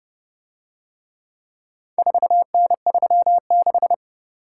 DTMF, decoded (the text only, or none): none